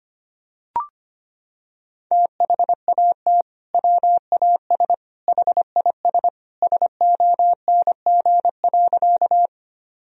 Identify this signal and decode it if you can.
Morse code, heard as a keyed tone, about 25 wpm, 700 Hz. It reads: THAT WAS HIS SONG.
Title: Sentences from Top 1000 Words - Review Part 2 - 25wpm